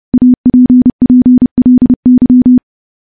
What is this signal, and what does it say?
Morse code, heard as a keyed tone, about 32 wpm, 250 Hz. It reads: APPLY
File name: morse / morse-training-30WPM